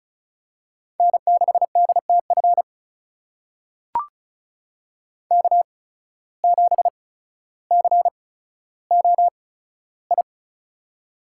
Morse code, heard as a keyed tone, about 36 wpm, spaced as for 15 wpm, 700 Hz. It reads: N6DTF K7COI
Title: USA Call Signs - 5 Characters - 8x Character Spacing - 35wpm